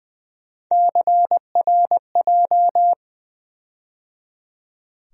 Morse code, heard as a keyed tone, about 20 wpm, 700 Hz. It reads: CRJ
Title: USA Call Sign Postfixes - 3 Letters 20wpm